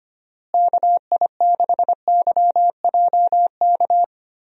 Morse code, heard as a keyed tone, about 25 wpm, 700 Hz. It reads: KI6YJK
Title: USA Call Signs - 6 Characters - Repeated 3x - 25wpm